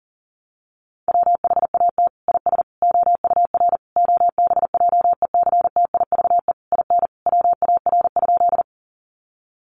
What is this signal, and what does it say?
Morse code, heard as a keyed tone, about 40 wpm, 700 Hz. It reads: WHAT IS OUR OBJECTIVE IN WAR?
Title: Sentences from Top 1200 Words 40wpm